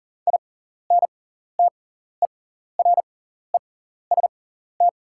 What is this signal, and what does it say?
Morse code, heard as a keyed tone, about 40 wpm, spaced as for 14 wpm, 690 Hz. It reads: INTEREST
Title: 6x Character Spacing - Top 500 Words - 40wpm